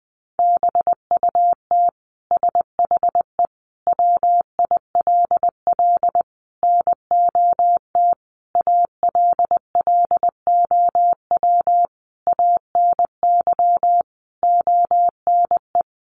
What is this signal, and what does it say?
Morse code, heard as a keyed tone, about 20 wpm, 700 Hz. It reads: BUT SHE WILL NOT ALLOW ANY ONE
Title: Sentences from Top 1200 Words 20wpm